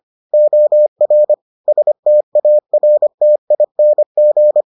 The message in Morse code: OR STARTING

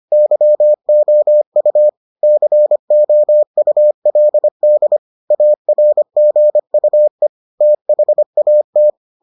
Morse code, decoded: YOU COULD ARGUE THAT

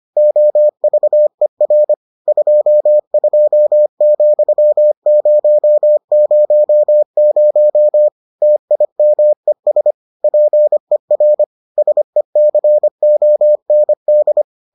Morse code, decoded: OVER 22,000 TIMES PER SECOND

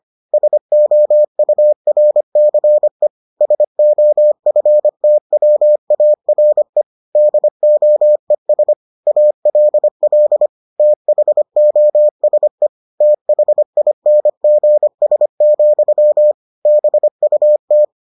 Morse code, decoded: SOURCE SOFTWARE DOES ALL THOSE THINGS, BUT

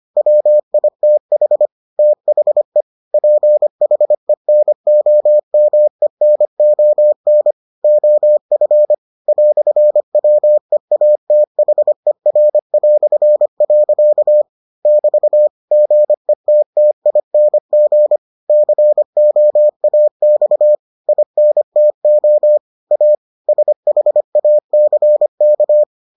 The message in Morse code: WITH THE PHENOMENON OF "WEATHER". <BT> GETTING COAX INTO A SHACK

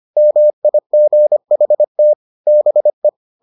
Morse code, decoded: MIGHT BE

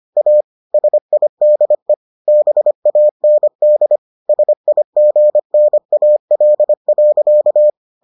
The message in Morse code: A SIDE BAND SIGNAL.

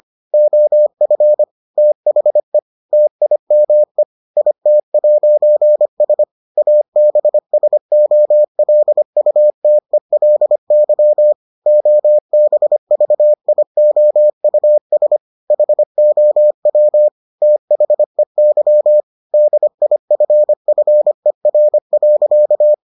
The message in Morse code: OF THE TIME IT'S ABSOLUTELY OBVIOUS HOW THEY DIFFER.

